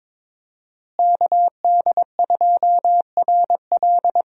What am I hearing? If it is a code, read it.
Morse code, heard as a keyed tone, about 22 wpm, 700 Hz. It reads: KD2RL
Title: USA Call Signs - 5 Characters - Repeated 2x - 22wpm